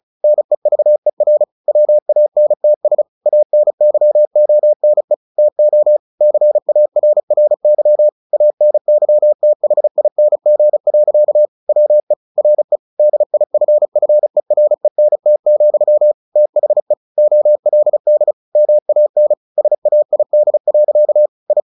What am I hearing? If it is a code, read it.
Morse code, heard as a keyed tone, about 35 wpm, 600 Hz. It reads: NEVER WANTS ANYONE TO CARRY ANYTHING. WE RE DIFFERENT, THE OLD MAN SAID. I
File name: the old man and the sea / the old man and the sea 1 35wpm